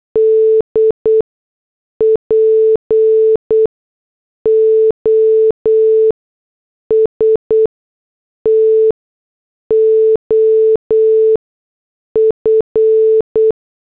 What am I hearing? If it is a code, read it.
Morse code, heard as a keyed tone, about 8 wpm, spaced as for 6 wpm, 440 Hz. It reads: D POST OF
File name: morse / headlines